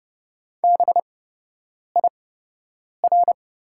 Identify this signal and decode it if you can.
Morse code, heard as a keyed tone, about 30 wpm, spaced as for 8 wpm, 700 Hz. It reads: BIR
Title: 8x Character Spacing - Top 500 Words - 30wpm